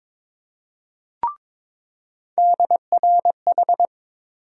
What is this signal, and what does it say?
Morse code, heard as a keyed tone, about 22 wpm, 700 Hz. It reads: DRH
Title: USA Call Sign Postfixes - 3 Letters 22wpm